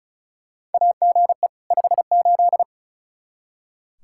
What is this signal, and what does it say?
Morse code, heard as a keyed tone, about 34 wpm, 700 Hz. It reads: AGE 58
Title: Practice QSO - Exchange 3 - More Info - 35wpm